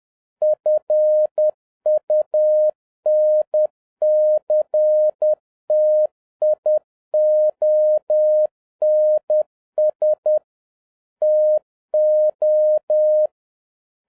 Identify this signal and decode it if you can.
Morse code, heard as a keyed tone, about 10 wpm, 610 Hz. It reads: FUNCTIONS TO
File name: 2011-10-mp3 / n11-MEETING WITH GOOD FRIENDS 10wpm